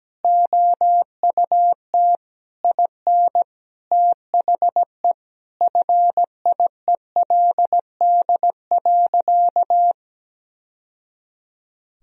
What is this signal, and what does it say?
Morse code, heard as a keyed tone, about 17 wpm, 700 Hz. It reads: OUT IN THE FIELD.
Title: Sentences from Top 1000 Words - Review Part 2 - 17wpm